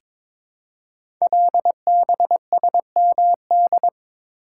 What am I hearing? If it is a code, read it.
Morse code, heard as a keyed tone, about 22 wpm, 700 Hz. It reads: LBSMD
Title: Code Groups - 5 Letters - 22wpm